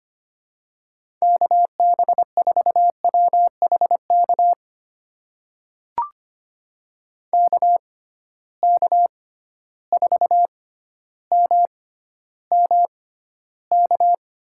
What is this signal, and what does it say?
Morse code, heard as a keyed tone, about 25 wpm, spaced as for 13 wpm, 700 Hz. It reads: KB4WHK KK4MMK
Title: USA Call Signs - 6 Characters - 6x Character Spacing - 25wpm